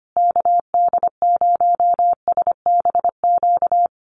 Morse code, heard as a keyed tone, about 25 wpm, 700 Hz. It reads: KD0SBQ
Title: USA Call Signs Repeated 3x 25wpm